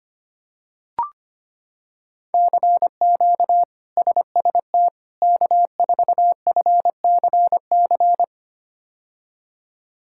Morse code, heard as a keyed tone, about 25 wpm, 700 Hz. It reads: CQ SST K4FCC
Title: K1USN SST QSO Practice - 25wpm